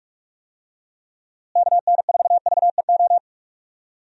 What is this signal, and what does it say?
Morse code, heard as a keyed tone, about 41 wpm, 700 Hz. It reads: KN4VEO
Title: USA Call Signs - 6 Characters - Repeated 3x - 45wpm